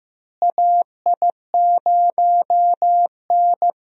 Morse code, heard as a keyed tone, about 15 wpm, 700 Hz. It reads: AI0N
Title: USA Call Signs - 4 Characters - Repeated 2x - 15wpm